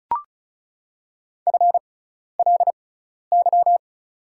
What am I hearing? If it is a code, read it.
Morse code, heard as a keyed tone, about 36 wpm, spaced as for 12 wpm, 700 Hz. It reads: FLY
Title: 6x Character Spacing - Top 500 Words - 35wpm